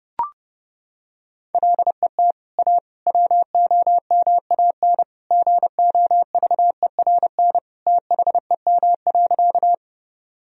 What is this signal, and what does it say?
Morse code, heard as a keyed tone, about 30 wpm, 700 Hz. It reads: LET A WOMAN GOVERN THEM.
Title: Sentences from Top 1000 Words - Review Part 2 - 30wpm